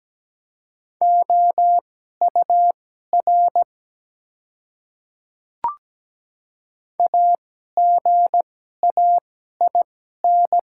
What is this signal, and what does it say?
Morse code, heard as a keyed tone, about 17 wpm, spaced as for 12 wpm, 700 Hz. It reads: OUR AGAIN
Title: 2x Character Spacing - Top 200 Words - 17wpm